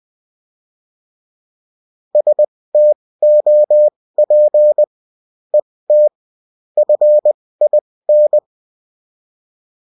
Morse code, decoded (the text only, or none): STOP ET FIN